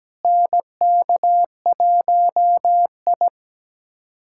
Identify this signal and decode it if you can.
Morse code, heard as a keyed tone, about 17 wpm, 700 Hz. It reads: NK1I
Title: USA Callsign DE Callsign - 4 Characters - 17wpm